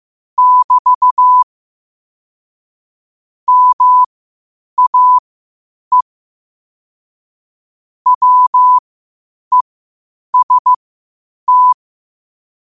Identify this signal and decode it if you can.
Morse code, heard as a keyed tone, about 15 wpm, spaced as for 8 wpm, 1000 Hz. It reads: <BT> MAE WEST
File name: qotdfiles / 8wpm